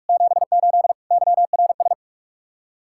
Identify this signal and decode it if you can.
Morse code, heard as a keyed tone, about 59 wpm, 700 Hz. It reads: 78 YRS